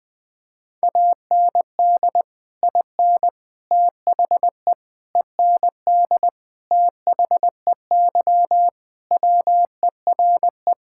Morse code, heard as a keyed tone, about 20 wpm, 700 Hz. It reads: AND IN THE END THEY WERE
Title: Sentences with 2-Word Phrases - Spoken Beginning - 20wpm